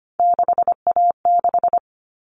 Morse code, heard as a keyed tone, about 25 wpm, 700 Hz. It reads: 6A6
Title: All Cut Numbers - 3-Digit 25wpm